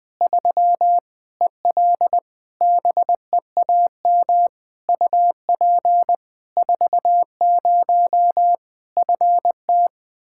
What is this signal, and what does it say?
Morse code, heard as a keyed tone, about 20 wpm, 700 Hz. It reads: 3 EL BEAM UP 40 FT
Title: Practice QSO - Exchange 2 - Info - 20wpm